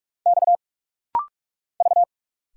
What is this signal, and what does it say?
Morse code, heard as a keyed tone, about 46 wpm, 700 Hz. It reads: X V
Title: Single Letters - Rapid-Fire 45wpm